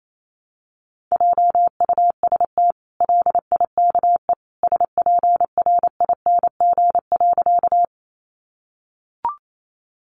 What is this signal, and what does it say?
Morse code, heard as a keyed tone, about 28 wpm, 700 Hz. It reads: JUST LIKE SPRING.
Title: Sentences from Top 1200 Words 28wpm